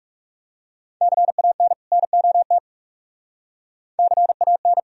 Morse code, heard as a keyed tone, about 44 wpm, 700 Hz. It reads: CAN NOT CAN